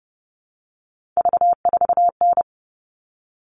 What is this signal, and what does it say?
Morse code, heard as a keyed tone, about 30 wpm, 700 Hz. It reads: V4N
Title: All Cut Numbers - 3-Digit 30wpm